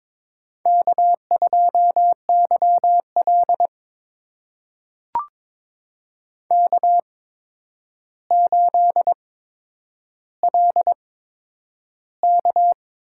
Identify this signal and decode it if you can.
Morse code, heard as a keyed tone, about 22 wpm, spaced as for 9 wpm, 700 Hz. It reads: K2YL K8LK
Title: USA Call Signs - 4 Characters - 8x Character Spacing - 22wpm